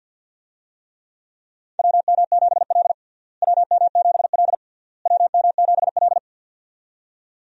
Morse code, heard as a keyed tone, about 52 wpm, 700 Hz. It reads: WM7L WM7L WM7L
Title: USA Call Signs - 4 Characters - Repeated 3x - 50wpm